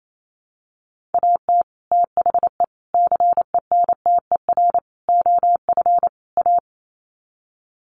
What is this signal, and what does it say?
Morse code, heard as a keyed tone, about 28 wpm, 700 Hz. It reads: AT THE CENTER OF A
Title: Sets of 5 Words Encore2 28wpm